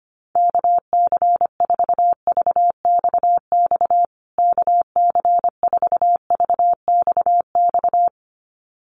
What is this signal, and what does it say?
Morse code, heard as a keyed tone, about 25 wpm, 700 Hz. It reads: KC4VXX KC4VXX
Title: USA Call Signs Repeated 2x 25wpm